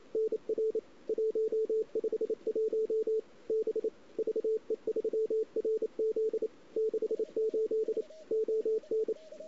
Morse code, heard as a keyed tone, about 31 wpm, 410 Hz, with radio noise. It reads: NR 151 B VE3RZ 68 ON
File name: CW / 2008-ss-cw-k5tr-op-n5rz -01:00